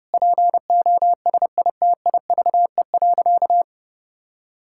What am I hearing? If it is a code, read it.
Morse code, heard as a keyed tone, about 30 wpm, 700 Hz. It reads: POSITIVE.
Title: Sentences from Top 1000 Words - Review Part 5 - 30wpm